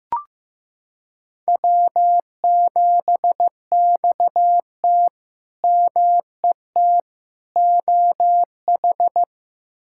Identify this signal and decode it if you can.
Morse code, heard as a keyed tone, about 15 wpm, 700 Hz. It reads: W7XT MET OH
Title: K1USN SST QSO Practice - 15wpm